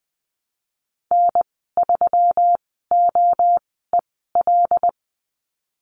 Morse code, heard as a keyed tone, about 20 wpm, spaced as for 14 wpm, 700 Hz. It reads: N3OEL
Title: 179 Lesson 29g - Slash - Call Signs 20wpm